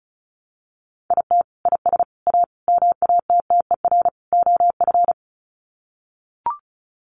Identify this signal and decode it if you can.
Morse code, heard as a keyed tone, about 34 wpm, 700 Hz. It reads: IT IS A MATTER OF
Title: Sets of 5 Words Encore2 35wpm